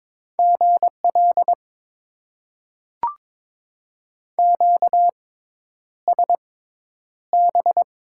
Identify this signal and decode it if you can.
Morse code, heard as a keyed tone, about 22 wpm, spaced as for 10 wpm, 700 Hz. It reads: GL QSB
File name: Top 90 QSO Elements - 6x Character Spacing - 22wpm